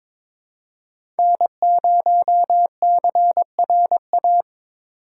Morse code, heard as a keyed tone, about 22 wpm, 700 Hz. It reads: N0CRA
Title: USA Call Signs - 2x Character Spacing - 22wpm